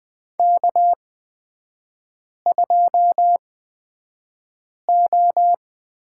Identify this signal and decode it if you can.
Morse code, heard as a keyed tone, about 20 wpm, spaced as for 5 wpm, 700 Hz. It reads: K2O
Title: 250 Lesson 40e - BK - Three Characters 20wpm